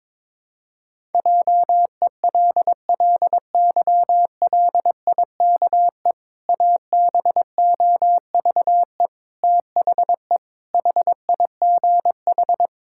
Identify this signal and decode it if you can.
Morse code, heard as a keyed tone, about 22 wpm, 700 Hz. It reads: JELLYLIKE ABOVE THE HIGH
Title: Pangram Challenge v2 - 22wpm